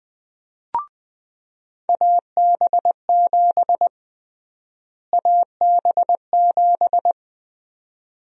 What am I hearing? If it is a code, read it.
Morse code, heard as a keyed tone, about 20 wpm, 700 Hz. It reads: AB7 AB7
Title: USA Call Sign Prefixes - Repeated 2x - 20wpm